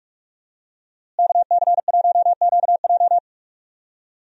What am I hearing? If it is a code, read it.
Morse code, heard as a keyed tone, about 46 wpm, 700 Hz. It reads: KC1QJ
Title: USA Call Signs - 5 Characters - Repeated 2x - 45wpm